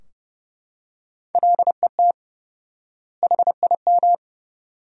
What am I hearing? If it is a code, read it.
Morse code, heard as a keyed tone, about 32 wpm, 700 Hz. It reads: LET HIM
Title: 4x Word-Spacing - Sentences from Top 500 Words - 30wpm